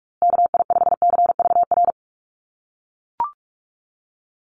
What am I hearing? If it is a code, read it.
Morse code, heard as a keyed tone, about 44 wpm, 700 Hz. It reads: KI5CVR
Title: USA Call Signs Repeated 2x 45wpm